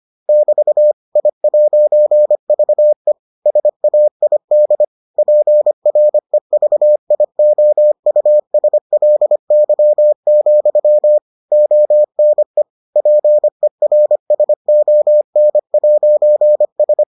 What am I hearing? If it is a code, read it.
Morse code, heard as a keyed tone, about 25 wpm, 600 Hz. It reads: <BT> I'VE SAID PREVIOUSLY, ONE PERSON'S